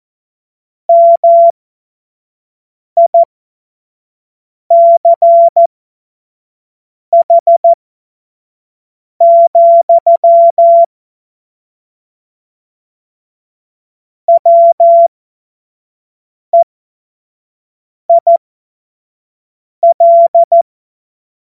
Morse code, heard as a keyed tone, about 14 wpm, spaced as for 5 wpm, 680 Hz. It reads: MICH, WEIL
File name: Morsekurs / Witze / Witz 15. November 2022 25 F70